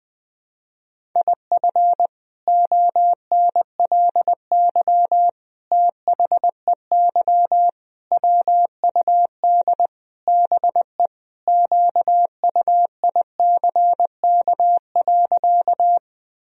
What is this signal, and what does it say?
Morse code, heard as a keyed tone, about 20 wpm, 700 Hz. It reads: IF ONLY THEY WUD BE QUICK.
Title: Sentences from Top 500 Words and Abbreviations 20wpm